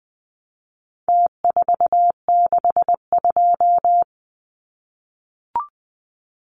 Morse code, heard as a keyed tone, about 20 wpm, 700 Hz. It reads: T462
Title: Numbers - Contest Style - 4-Digit 20wpm